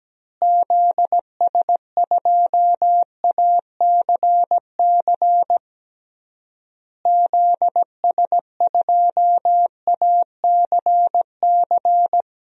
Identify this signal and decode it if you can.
Morse code, heard as a keyed tone, about 17 wpm, 700 Hz. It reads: ZS2ACC ZS2ACC
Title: DXCC-100 Call Signs - 6 Characters - Repeated 3x - 17wpm